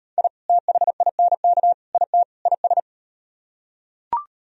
Morse code, heard as a keyed tone, about 38 wpm, 700 Hz. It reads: I THINK IT IS